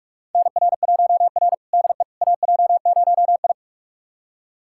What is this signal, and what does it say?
Morse code, heard as a keyed tone, about 46 wpm, 700 Hz. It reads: NR1R DE AJ0I
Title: USA Callsign DE Callsign - 4 Characters - 45wpm